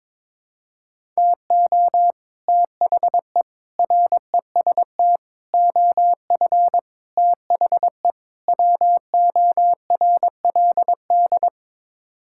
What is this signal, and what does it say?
Morse code, heard as a keyed tone, about 22 wpm, 700 Hz. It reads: TO THE REST OF THE WORLD
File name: practice / 22wpm / Sets of 6 Words - 22wpm